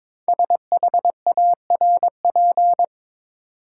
Morse code, heard as a keyed tone, about 22 wpm, 700 Hz. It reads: SHARP
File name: 5 Letter Words - First 3 Letters Given - 22wpm